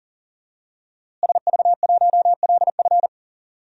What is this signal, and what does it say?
Morse code, heard as a keyed tone, about 40 wpm, 700 Hz. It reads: SV1LF
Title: DXCC-100 Call Signs - 5 Characters - Repeated 3x - 40wpm